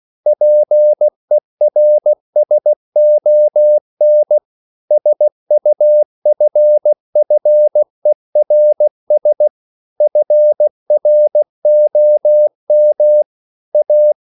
Morse code, heard as a keyed tone, about 16 wpm, 600 Hz. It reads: PERSON SUFFERS FROM A
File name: CW / Practice / EnglishQuotations0116